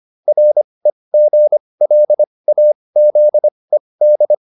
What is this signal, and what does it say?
Morse code, heard as a keyed tone, about 25 wpm, spaced as for 20 wpm, 600 Hz. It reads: REGLAZED